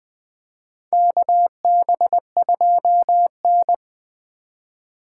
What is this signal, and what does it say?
Morse code, heard as a keyed tone, about 21 wpm, 700 Hz. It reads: KB2N
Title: USA Call Signs - 3x Character Spacing - 20wpm